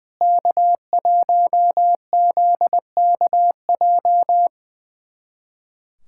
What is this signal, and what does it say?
Morse code, heard as a keyed tone, about 21 wpm, 700 Hz. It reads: K1ZKJ
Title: USA Callsign DE Callsign 20wpm